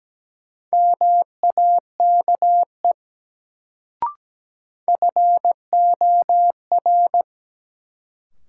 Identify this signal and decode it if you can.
Morse code, heard as a keyed tone, about 17 wpm, 700 Hz. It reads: MAKE FOR
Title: Sentences from Top 100 Words - New Word Drill - 17wpm